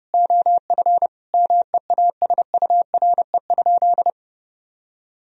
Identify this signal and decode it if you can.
Morse code, heard as a keyed tone, about 30 wpm, 700 Hz. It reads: OF MEASURE?
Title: Sentences from Top 1000 Words - Review Part 2 - 30wpm